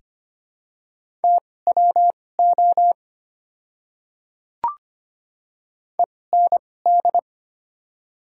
Morse code, heard as a keyed tone, about 25 wpm, spaced as for 18 wpm, 700 Hz. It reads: TWO END
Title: 2x Character Spacing - Top 300 Words - 25wpm